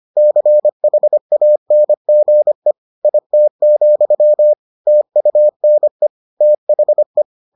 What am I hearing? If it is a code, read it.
Morse code, heard as a keyed tone, about 25 wpm, 600 Hz. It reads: CHANGE IT, TUNE THE